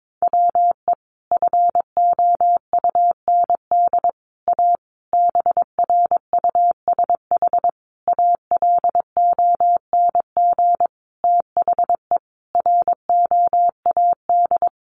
Morse code, heard as a keyed tone, about 22 wpm, 700 Hz. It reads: WE FOUND A BRUSH ALONG THE ROAD
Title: Sentences from Top 1100 Words 22wpm